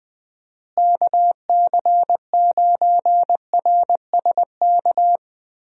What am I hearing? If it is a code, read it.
Morse code, heard as a keyed tone, about 20 wpm, 700 Hz. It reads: KC9RSK
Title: USA Call Signs - 6 Characters - Repeated 2x - 20wpm